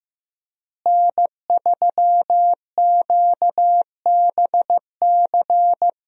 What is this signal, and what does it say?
Morse code, heard as a keyed tone, about 15 wpm, 700 Hz. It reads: N3QBC
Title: USA Call Signs - 5 Characters - Repeated 2x - 15wpm